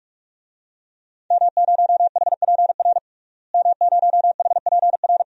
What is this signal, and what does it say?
Morse code, heard as a keyed tone, about 45 wpm, 700 Hz. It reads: M0HPR M0HPR
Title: DXCC-100 Call Signs - 5 Characters - Repeated 2x - 45wpm